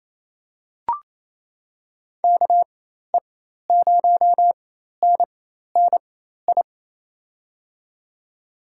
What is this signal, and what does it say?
Morse code, heard as a keyed tone, about 27 wpm, spaced as for 13 wpm, 700 Hz. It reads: KE0NNI
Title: USA Call Signs - 6 Characters - 4x Character Spacing - 28wpm